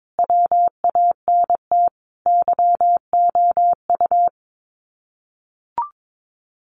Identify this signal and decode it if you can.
Morse code, heard as a keyed tone, about 22 wpm, 700 Hz. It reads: WANT YOU
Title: Sets of 5 Words 22wpm